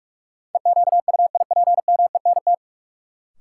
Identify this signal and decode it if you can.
Morse code, heard as a keyed tone, about 46 wpm, 700 Hz. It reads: EQUIPMENT
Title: Sentences from Top 700 Words - New Word Drill - 45wpm